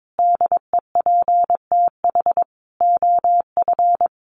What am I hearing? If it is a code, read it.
Morse code, heard as a keyed tone, about 22 wpm, 700 Hz. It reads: DEPTH OF
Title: Sentences from Top 1200 Words 22wpm